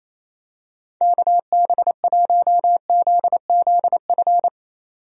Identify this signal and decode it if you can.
Morse code, heard as a keyed tone, about 27 wpm, 700 Hz. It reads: KB1ZZF
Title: USA Call Signs - 6 Characters - 28wpm